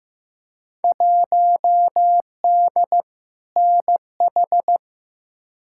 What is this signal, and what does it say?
Morse code, heard as a keyed tone, about 15 wpm, 700 Hz. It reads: 1D NH
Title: Field Day Exchanges - 15wpm